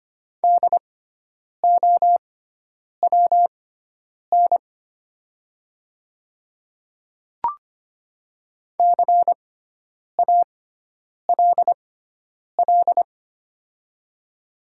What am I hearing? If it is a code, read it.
Morse code, heard as a keyed tone, about 25 wpm, spaced as for 9 wpm, 700 Hz. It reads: DOWN CALL
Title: 6x Character Spacing - Top 300 Words - 25wpm